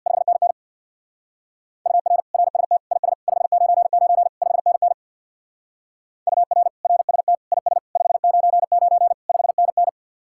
Morse code, heard as a keyed tone, about 59 wpm, 700 Hz. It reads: 5NN UR RST IS 599 5NN UR RST IS 599 5NN
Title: Speed-Racing Common QSO Phrases Encore - 50wpm